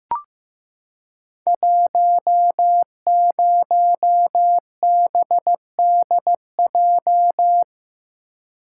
Morse code, heard as a keyed tone, about 15 wpm, 700 Hz. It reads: 10BDJ
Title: Code Groups - 5 Letter:Numbers - 15wpm